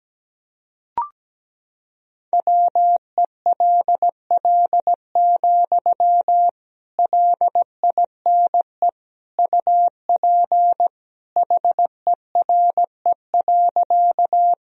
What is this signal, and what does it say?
Morse code, heard as a keyed tone, about 17 wpm, 700 Hz. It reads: WELL, LINE UP HERE.
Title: Sentences from Top 1000 Words - Review Part 1 - 17wpm